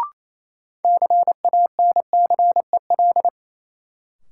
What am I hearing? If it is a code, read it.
Morse code, heard as a keyed tone, about 28 wpm, 700 Hz. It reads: CANCEL
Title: Sentences from Top 1200 Words - New Word Drill - 28wpm